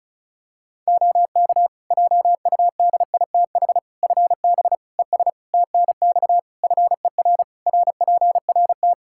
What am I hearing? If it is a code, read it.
Morse code, heard as a keyed tone, about 36 wpm, 700 Hz. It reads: OK JUDITH FB ES TNX FER RPRT
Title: Practice QSO - Exchange 2 - Info - 35wpm